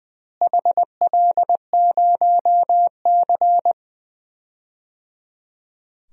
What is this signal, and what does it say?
Morse code, heard as a keyed tone, about 20 wpm, 700 Hz. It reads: HL0C
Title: DXCC-100 Call Signs - 4 Characters - Repeated 3x - 20wpm